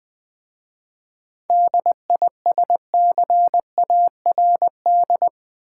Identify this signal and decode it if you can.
Morse code, heard as a keyed tone, about 20 wpm, 700 Hz. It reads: DISCARD
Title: 7 Letter Words - First 3 Letters Given - 20wpm